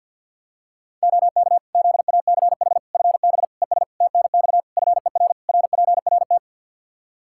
Morse code, heard as a keyed tone, about 52 wpm, 700 Hz. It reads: OK ZACH FB ES TNX FER RPRT
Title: K1USN SST QSO Practice 50wpm